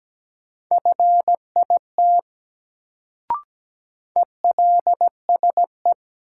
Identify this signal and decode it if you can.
Morse code, heard as a keyed tone, about 17 wpm, 700 Hz. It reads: FIT ELSE
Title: Sentences from Top 800 Words - New Word Drill - 17wpm